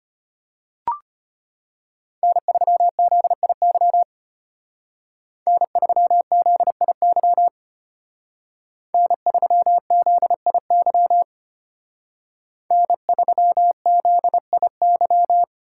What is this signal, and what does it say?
Morse code, heard as a keyed tone, about 32 wpm, 700 Hz. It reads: N3ZIY N3ZIY N3ZIY N3ZIY